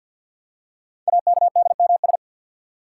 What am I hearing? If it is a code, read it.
Morse code, heard as a keyed tone, about 48 wpm, 700 Hz. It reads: AKDMS
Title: Code Groups - 5 Letters - 50wpm